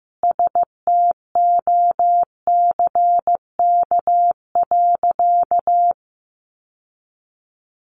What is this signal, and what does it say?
Morse code, heard as a keyed tone, about 14 wpm, 700 Hz. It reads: STOCK.
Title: Sentences from Top 1300 Words 15wpm